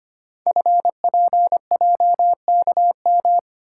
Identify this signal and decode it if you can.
Morse code, heard as a keyed tone, about 25 wpm, 700 Hz. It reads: FPJKM
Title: Code Groups - 5 Letter:Numbers - 25wpm